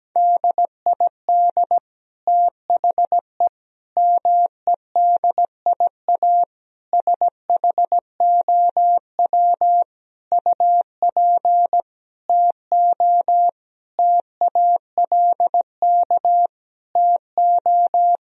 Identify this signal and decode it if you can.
Morse code, heard as a keyed tone, about 17 wpm, 700 Hz. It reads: DID THE MEDIA SHOW UP TO TALK TO
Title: Sentences from Top 1000 Words - Review Part 4 - 17wpm